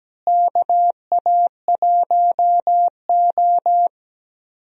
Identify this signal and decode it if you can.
Morse code, heard as a keyed tone, about 17 wpm, 700 Hz. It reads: KA1O
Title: USA Call Signs - 4 Characters - Repeated 3x - 17wpm